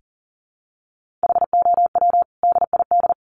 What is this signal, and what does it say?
Morse code, heard as a keyed tone, about 40 wpm, 700 Hz. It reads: HOW DID
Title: Sentences from Top 1500 Words 40wpm